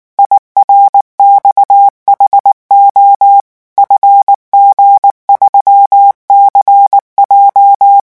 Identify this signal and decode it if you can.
Morse code, heard as a keyed tone, about 19 wpm, 800 Hz. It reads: IRXHO FG3CJ
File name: morse mp3 / revision10-vitesse 18 mots